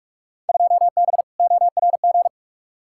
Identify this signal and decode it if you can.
Morse code, heard as a keyed tone, about 42 wpm, 700 Hz. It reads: 2B ORG
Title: Field Day Exchanges Encore - 45wpm